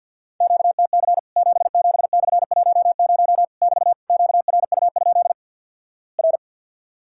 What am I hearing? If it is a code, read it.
Morse code, heard as a keyed tone, about 50 wpm, 700 Hz. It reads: QTC 77/10 <BT> QRU? R
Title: Worked All Europe DX Contest - QTC Exchange - 50wpm